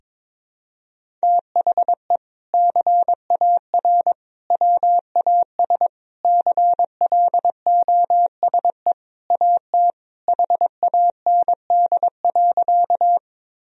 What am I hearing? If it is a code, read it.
Morse code, heard as a keyed tone, about 22 wpm, 700 Hz. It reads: THE CAR WAS CLOSE AT HAND.
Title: Sentences from Top 1000 Words - Review Part 1 - 22wpm